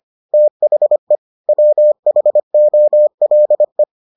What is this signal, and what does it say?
Morse code, heard as a keyed tone, about 25 wpm, 600 Hz. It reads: THE WHOLE